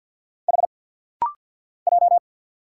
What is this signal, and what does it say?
Morse code, heard as a keyed tone, about 50 wpm, 700 Hz. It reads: H J
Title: Single Letters - Rapid-Fire 50wpm